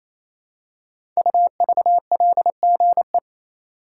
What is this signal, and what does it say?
Morse code, heard as a keyed tone, about 27 wpm, 700 Hz. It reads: UVLGE
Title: Code Groups - 5 Letter:Numbers - 28wpm